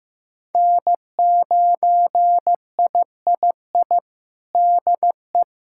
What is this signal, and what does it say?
Morse code, heard as a keyed tone, about 15 wpm, 700 Hz. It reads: N9III DE
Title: Practice QSO - Exchange 4 - Ending - 15wpm